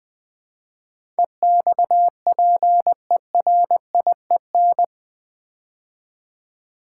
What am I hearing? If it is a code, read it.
Morse code, heard as a keyed tone, about 20 wpm, 700 Hz. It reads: EXPERIEN
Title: Build A Word - 4 Letters At A Time - 11-Letter Words - 20wpm